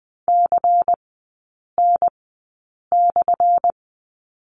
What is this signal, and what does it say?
Morse code, heard as a keyed tone, about 19 wpm, spaced as for 8 wpm, 700 Hz. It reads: CN/
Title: 178 Lesson 29e - Slash - Three Characters 20wpm